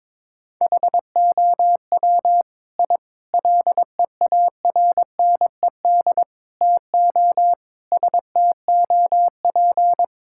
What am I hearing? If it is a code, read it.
Morse code, heard as a keyed tone, about 22 wpm, 700 Hz. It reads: HOW I LEARNED TO STOP
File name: practice / 22wpm / Sentences with 8-Word Phrases - 22wpm